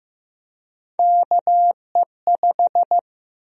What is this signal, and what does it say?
Morse code, heard as a keyed tone, about 15 wpm, 700 Hz. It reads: KE5
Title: USA Call Sign Prefixes - Repeated 2x - 15wpm